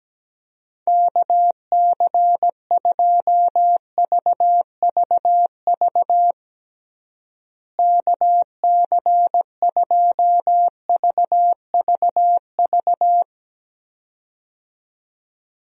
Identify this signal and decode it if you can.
Morse code, heard as a keyed tone, about 17 wpm, 700 Hz. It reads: KC2VVV KC2VVV
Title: USA Call Signs - 6 Characters - Repeated 2x - 17wpm